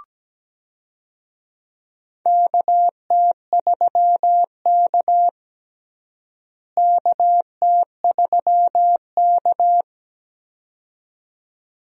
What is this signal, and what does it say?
Morse code, heard as a keyed tone, about 17 wpm, 700 Hz. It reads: KT3K KT3K
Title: USA Call Signs - 4 Characters - Repeated 2x - 17wpm